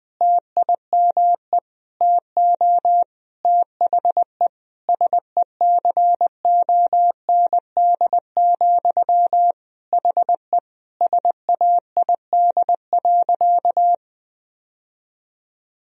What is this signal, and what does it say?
Morse code, heard as a keyed tone, about 20 wpm, 700 Hz. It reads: TIME TO THE SECOND, HE SAID.
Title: Sentences from Top 1000 Words - Review Part 1 - 20wpm